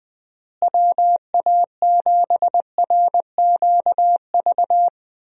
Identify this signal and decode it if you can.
Morse code, heard as a keyed tone, about 20 wpm, 700 Hz. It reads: WA7RQV
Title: USA Call Signs - 6 Characters - Repeated 3x - 20wpm